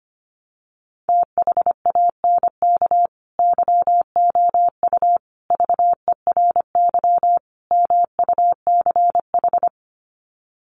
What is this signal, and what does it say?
Morse code, heard as a keyed tone, about 25 wpm, 700 Hz. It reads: THANK YOU VERY MUCH
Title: Sets of 4 Words 25wpm